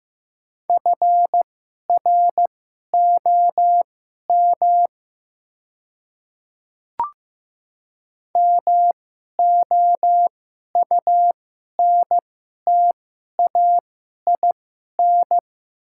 Morse code, heard as a keyed tone, about 15 wpm, spaced as for 11 wpm, 690 Hz. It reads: FROM MOUNTAIN
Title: 2x Character Spacing - Top 500 Words - 15wpm